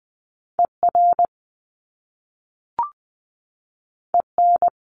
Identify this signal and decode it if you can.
Morse code, heard as a keyed tone, about 20 wpm, 700 Hz. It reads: ER EN
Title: Top Two Letter Combinations in Words 20wpm